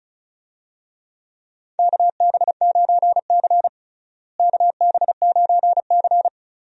Morse code, heard as a keyed tone, about 36 wpm, 700 Hz. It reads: KB9C KB9C
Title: USA Call Signs - 4 Characters - Repeated 3x - 35wpm